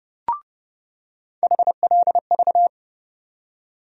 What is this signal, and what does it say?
Morse code, heard as a keyed tone, about 30 wpm, 700 Hz. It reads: HLV
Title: USA Call Sign Postfixes - 3 Letters 30wpm